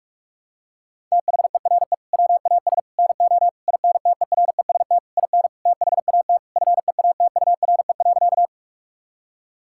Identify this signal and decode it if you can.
Morse code, heard as a keyed tone, about 46 wpm, 700 Hz. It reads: THERE WAS NO INTEREST IN THAT FEATURE.
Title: Sentences from Top 1000 Words - Review Part 5 - 45wpm